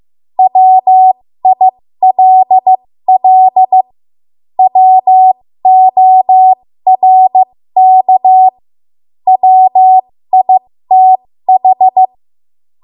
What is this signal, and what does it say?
Morse code, heard as a keyed tone, about 15 wpm, spaced as for 13 wpm, 750 Hz. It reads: WILL WORK WITH
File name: Morse / 13WPM